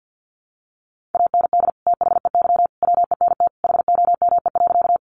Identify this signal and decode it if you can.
Morse code, heard as a keyed tone, about 50 wpm, 700 Hz. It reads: AND THEY WENT HOME.
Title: Sentences from Top 200 Words 50wpm